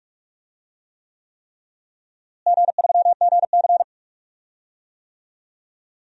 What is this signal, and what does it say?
Morse code, heard as a keyed tone, about 48 wpm, 700 Hz. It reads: G3GC
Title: DXCC-100 Call Signs - 4 Characters - Repeated 3x - 45wpm